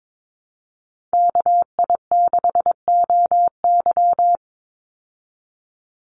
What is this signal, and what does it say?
Morse code, heard as a keyed tone, about 22 wpm, 700 Hz. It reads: KI6OY
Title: Call Signs Repeated 3x - 22wpm